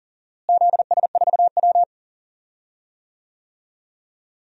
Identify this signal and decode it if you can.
Morse code, heard as a keyed tone, about 40 wpm, 700 Hz. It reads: ZS4W
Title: DXCC-100 Call Signs - 4 Characters - Repeated 3x - 40wpm